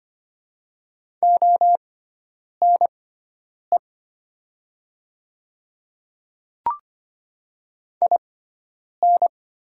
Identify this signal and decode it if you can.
Morse code, heard as a keyed tone, about 25 wpm, spaced as for 9 wpm, 700 Hz. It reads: ONE IN
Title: 6x Character Spacing - Top 100 Words - 25wpm